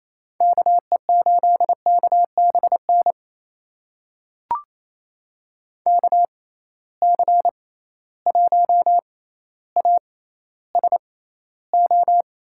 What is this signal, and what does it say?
Morse code, heard as a keyed tone, about 28 wpm, spaced as for 14 wpm, 700 Hz. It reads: KE8KBN KC1ASO
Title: USA Call Signs - 6x Character Spacing - 28wpm